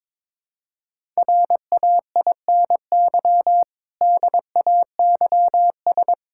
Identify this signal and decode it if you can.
Morse code, heard as a keyed tone, about 22 wpm, 700 Hz. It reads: RAINY DAYS
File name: Difficult Sentences with Hints - 22wpm